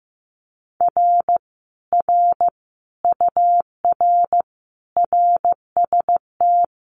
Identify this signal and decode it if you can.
Morse code, heard as a keyed tone, about 15 wpm, 700 Hz. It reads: R R UR RST
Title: Common QSO Phrases Encore 15wpm